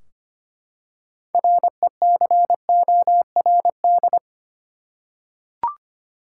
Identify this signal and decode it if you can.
Morse code, heard as a keyed tone, about 25 wpm, 700 Hz. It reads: RECORD
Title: Six Letter Words - 25wpm